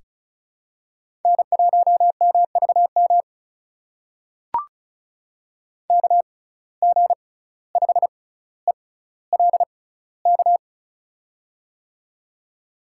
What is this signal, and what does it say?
Morse code, heard as a keyed tone, about 36 wpm, spaced as for 17 wpm, 700 Hz. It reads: N1MVM KG5ELK
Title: USA Call Signs - 6x Character Spacing - 35wpm